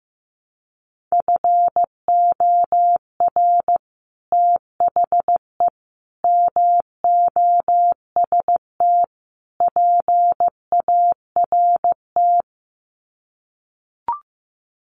Morse code, heard as a keyed tone, about 15 wpm, 700 Hz. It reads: FOR THE MOST PART
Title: Sets of 4 Words 15wpm